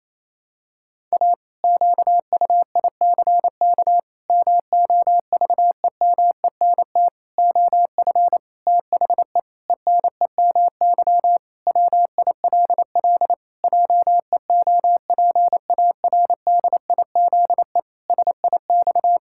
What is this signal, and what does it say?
Morse code, heard as a keyed tone, about 28 wpm, 700 Hz. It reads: A QUICK MOVEMENT OF THE ENEMY WILL JEOPARDIZE SIX
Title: Pangram Challenge v2 - 28wpm